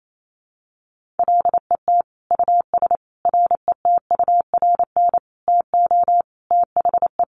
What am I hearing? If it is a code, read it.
Morse code, heard as a keyed tone, about 28 wpm, 700 Hz. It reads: LET US RETURN TO THE
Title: Sentences from Top 1500 Words 28wpm